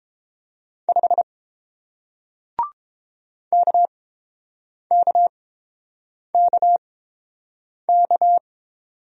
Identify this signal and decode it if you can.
Morse code, heard as a keyed tone, about 30 wpm, spaced as for 7 wpm, 700 Hz. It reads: 5 KKKK